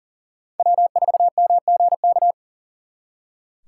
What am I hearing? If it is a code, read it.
Morse code, heard as a keyed tone, about 40 wpm, 700 Hz. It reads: W4MGK
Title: Speed Racing - USA 1x3 Call Signs 40wpm